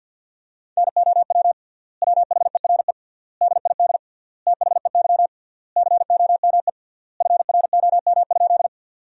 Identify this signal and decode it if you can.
Morse code, heard as a keyed tone, about 53 wpm, 700 Hz. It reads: NOW WHERE DID THEY COME FROM?
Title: 3x Word-Spacing - Sentences from Top 200 Words - 50wpm